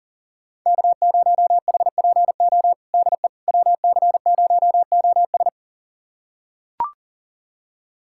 Morse code, heard as a keyed tone, about 40 wpm, 700 Hz. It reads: K0HPO DE WC0OS
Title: USA Callsign DE Callsign - 5 Characters - 40wpm